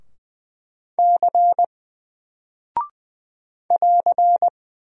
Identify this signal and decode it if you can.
Morse code, heard as a keyed tone, about 20 wpm, 700 Hz. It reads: C <AR>
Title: Single Character (Letter:Number:Punctuation:Prosign) Alt 20wpm